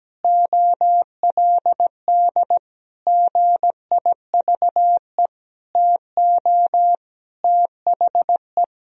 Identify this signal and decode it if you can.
Morse code, heard as a keyed tone, about 17 wpm, 700 Hz. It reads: OLD GIVE TO THE
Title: Sentences from Top 1000 Words - Review Part 1 - 17wpm